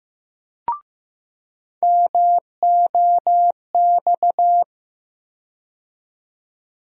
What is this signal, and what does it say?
Morse code, heard as a keyed tone, about 15 wpm, 700 Hz. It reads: MOX
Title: USA Call Sign Postfixes - 3 Letters 15wpm